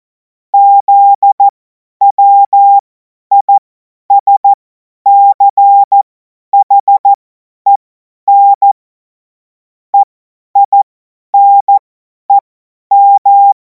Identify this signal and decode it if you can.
Morse code, heard as a keyed tone, about 14 wpm, spaced as for 10 wpm, 800 Hz. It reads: ZWISCHEN EINEM
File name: Morsekurs / Witze / Witz 07. Juli 2026 50 F70